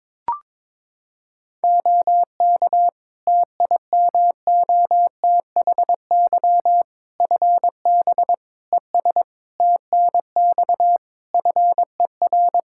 Morse code, heard as a keyed tone, about 22 wpm, 700 Hz. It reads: OK TIMOTHY FB ES TNX FER
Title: K1USN SST QSO Practice 22wpm